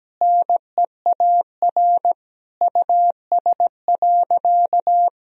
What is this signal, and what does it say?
Morse code, heard as a keyed tone, about 17 wpm, 700 Hz. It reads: NEAR US.
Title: Sentences from Top 1000 Words - Review Part 1 - 17wpm